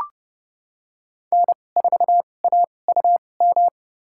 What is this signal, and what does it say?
Morse code, heard as a keyed tone, about 30 wpm, spaced as for 22 wpm, 700 Hz. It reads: N4AUM
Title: USA Call Signs - 5 Characters - 2x Character Spacing - 30wpm